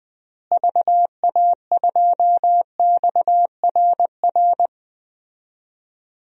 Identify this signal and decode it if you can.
Morse code, heard as a keyed tone, about 20 wpm, 700 Hz. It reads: VA2XRR
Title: DXCC-100 Call Signs - 6 Characters - Repeated 3x - 20wpm